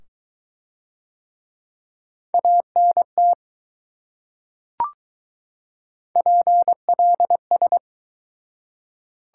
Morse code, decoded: ANT PLS

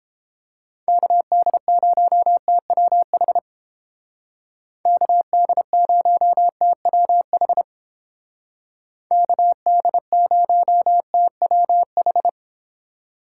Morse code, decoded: KD0TWH KD0TWH KD0TWH

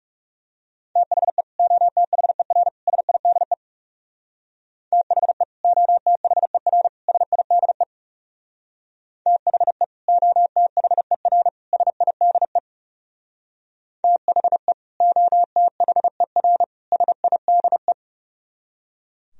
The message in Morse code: THE OTHER SIDE THE OTHER SIDE THE OTHER SIDE THE OTHER SIDE

